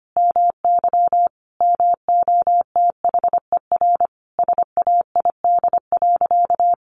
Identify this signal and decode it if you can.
Morse code, heard as a keyed tone, about 25 wpm, 700 Hz. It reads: MY MOTHER SAID.
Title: Sentences from Top 1400 Words 25wpm